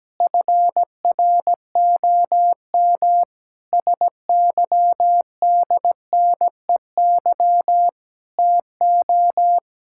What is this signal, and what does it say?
Morse code, heard as a keyed tone, about 17 wpm, 700 Hz. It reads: FROM SYDNEY TO
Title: Pangram Challenge v2 - 17wpm